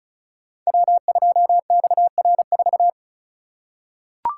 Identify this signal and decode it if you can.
Morse code, heard as a keyed tone, about 36 wpm, 700 Hz. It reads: W2XR4
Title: Code Groups - 5 Letter:Numbers - 35wpm